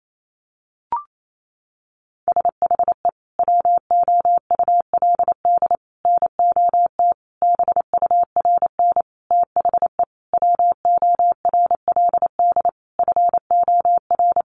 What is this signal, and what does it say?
Morse code, heard as a keyed tone, about 28 wpm, 700 Hz. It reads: SHE WOULD NOT BURN THE WORLD FOR
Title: Sentences from Top 1300 Words 28wpm